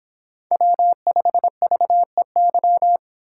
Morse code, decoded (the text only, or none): W5VEY